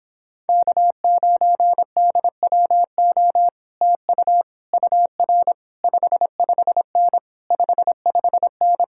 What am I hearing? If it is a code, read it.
Morse code, heard as a keyed tone, about 26 wpm, 700 Hz. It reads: K9DWO TU UR 55N 55N